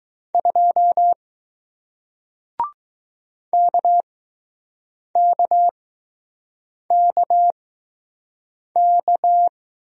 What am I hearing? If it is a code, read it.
Morse code, heard as a keyed tone, about 20 wpm, spaced as for 6 wpm, 700 Hz. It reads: 2 KKKK